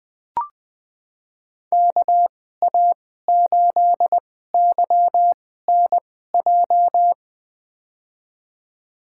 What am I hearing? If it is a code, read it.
Morse code, heard as a keyed tone, about 20 wpm, spaced as for 15 wpm, 700 Hz. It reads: KA8YNJ
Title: USA Call Signs - 6 Characters - 2x Character Spacing - 20wpm